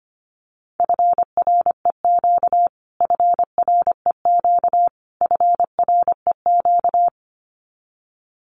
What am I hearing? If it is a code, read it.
Morse code, heard as a keyed tone, about 25 wpm, 700 Hz. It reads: FREQ FREQ FREQ
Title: Abbreviations Repeated 3x - 25wpm